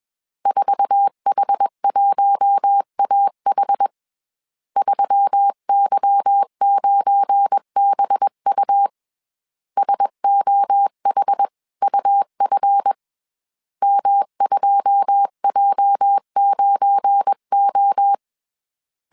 Morse code, heard as a keyed tone, about 21 wpm, 800 Hz. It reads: 4H1AH 3Y9BU SOHUF M2J9O
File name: CW / DET / 1 9 1500C S C 20wpm 2310